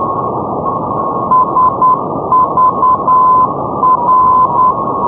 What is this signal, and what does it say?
Morse code, heard as a keyed tone, about 10 wpm, 1020 Hz, with radio noise. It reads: SVR
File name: europe / hng / hng-670 ndb sagvar svr-348+srb-78-tpl-348+hrv-240-sk-350-np-aut-poellau-2011-rb